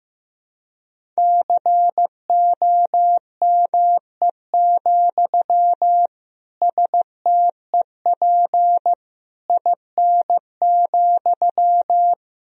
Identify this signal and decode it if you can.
Morse code, heard as a keyed tone, about 15 wpm, 700 Hz. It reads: COME, STEP IN,
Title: Sentences from Top 1000 Words - Review Part 2 - 15wpm